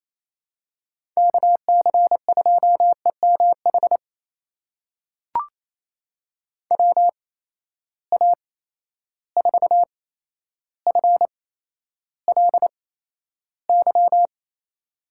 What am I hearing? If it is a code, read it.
Morse code, heard as a keyed tone, about 28 wpm, spaced as for 12 wpm, 700 Hz. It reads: KC2EMH WA4FLY
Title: USA Call Signs - 6 Characters - 8x Character Spacing - 28wpm